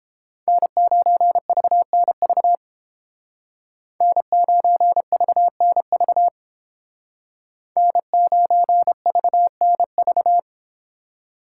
Morse code, heard as a keyed tone, about 30 wpm, 700 Hz. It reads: N9VNV N9VNV N9VNV